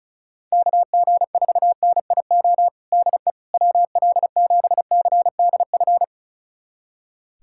Morse code, heard as a keyed tone, about 36 wpm, 700 Hz. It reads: KG4NIO DE WL7CDF
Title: USA Callsign DE Callsign - 6 Characters - 35wpm